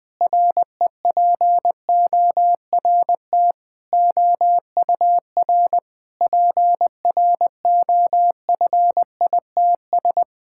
Morse code, heard as a keyed tone, about 20 wpm, 700 Hz. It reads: REPORT OUR PROFITS
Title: Sentences with 6-Word Phrases - 20wpm